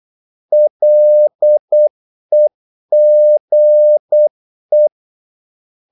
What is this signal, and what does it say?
Morse code, heard as a keyed tone, about 8 wpm, 600 Hz. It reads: LEGE
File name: Korte ord DK 8wpm